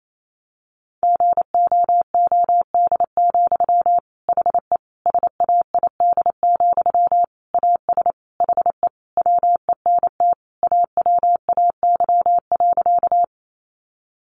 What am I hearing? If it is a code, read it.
Morse code, heard as a keyed tone, about 28 wpm, 700 Hz. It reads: GOOD, HE SAID, AS HE WENT AWAY.
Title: Sentences from Top 600 Words 28wpm